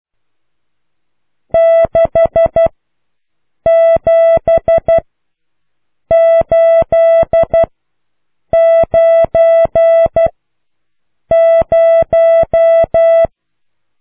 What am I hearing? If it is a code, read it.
Morse code, heard as a keyed tone, about 12 wpm, spaced as for 6 wpm, 660 Hz. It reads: 67890